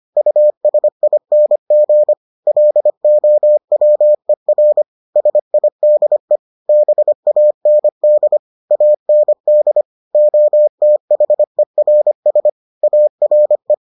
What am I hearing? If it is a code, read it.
Morse code, heard as a keyed tone, about 25 wpm, 600 Hz. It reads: USING LOWER SIDE BAND AND OTHERS ARE